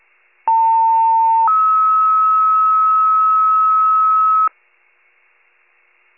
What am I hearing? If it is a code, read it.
Two-tone page, heard as a one-second tone, then a three-second tone, about 900 then 1290 Hz. It is heard from 0.5 to 4.5 s.
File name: pager / 78 RESCUE DUTY / wcalert